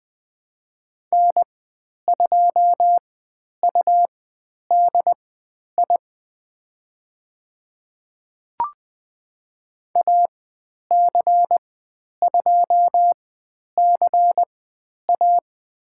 Morse code, heard as a keyed tone, about 20 wpm, spaced as for 10 wpm, 700 Hz. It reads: N2UDI AC2CA
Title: 102 Lesson 17g - Letter u - Call Signs 20wpm